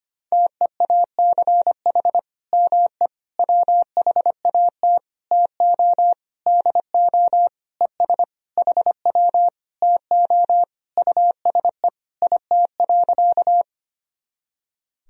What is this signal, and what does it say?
Morse code, heard as a keyed tone, about 25 wpm, 700 Hz. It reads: TEACH ME WHAT TO DO ES HW TO USE IT.
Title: Sentences from Top 500 Words and Abbreviations 25wpm